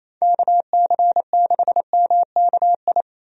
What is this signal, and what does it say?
Morse code, heard as a keyed tone, about 28 wpm, 700 Hz. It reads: KC6MKI
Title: USA Call Signs - 6 Characters - 28wpm